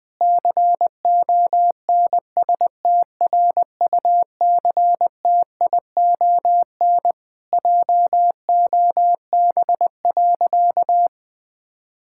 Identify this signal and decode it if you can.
Morse code, heard as a keyed tone, about 21 wpm, 700 Hz. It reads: CONSTRUCTION JOB.
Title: Sentences from Top 1600 Words 20wpm